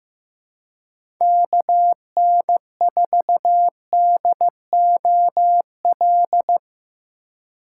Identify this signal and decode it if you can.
Morse code, heard as a keyed tone, about 15 wpm, 700 Hz. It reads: KN4DOL
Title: USA Call Signs - 6 Characters - 4x Character Spacing - 15wpm